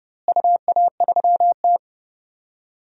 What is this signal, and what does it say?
Morse code, heard as a keyed tone, about 30 wpm, 700 Hz. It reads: UA3T
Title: DXCC-100 Call Signs - 4 Characters - Repeated 3x - 30wpm